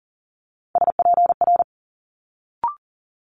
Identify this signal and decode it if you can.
Morse code, heard as a keyed tone, about 42 wpm, 700 Hz. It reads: SPR
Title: Multi-letter Phonemes Quick Fire 40wpm